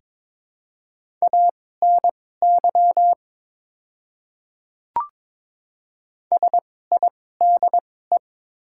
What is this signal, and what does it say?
Morse code, heard as a keyed tone, about 22 wpm, spaced as for 16 wpm, 700 Hz. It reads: ANY SIDE
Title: 2x Character Spacing - Top 200 Words - 22wpm